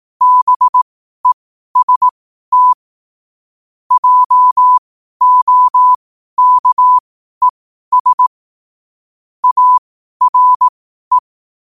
Morse code, heard as a keyed tone, about 18 wpm, spaced as for 12 wpm, 1000 Hz. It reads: BEST JOKES ARE
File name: qotdfiles / 12wpm